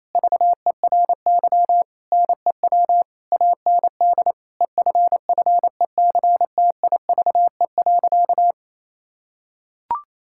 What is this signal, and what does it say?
Morse code, heard as a keyed tone, about 28 wpm, 700 Hz. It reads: VERY NEW AND EFFECTIVE.
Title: Sentences from Top 1000 Words - Review Part 6 - 28wpm